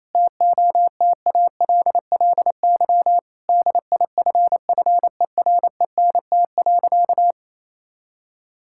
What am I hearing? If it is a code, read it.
Morse code, heard as a keyed tone, about 28 wpm, 700 Hz. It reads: TOTALLY DIFFERENT.
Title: Sentences from Top 1600 Words 28wpm